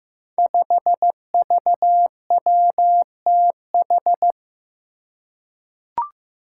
Morse code, heard as a keyed tone, about 15 wpm, 700 Hz. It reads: 5VWTH
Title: Five Challenging Letter-Number Combinations 15wpm